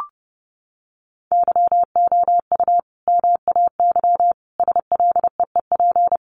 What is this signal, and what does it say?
Morse code, heard as a keyed tone, about 28 wpm, 700 Hz. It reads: YOU MAY SLEEP
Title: Sentences from Top 1500 Words 30wpm